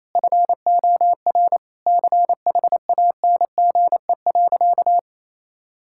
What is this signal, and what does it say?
Morse code, heard as a keyed tone, about 28 wpm, 700 Hz. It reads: FOR CHANGE.
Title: Sentences from Top 1000 Words - Review Part 6 - 28wpm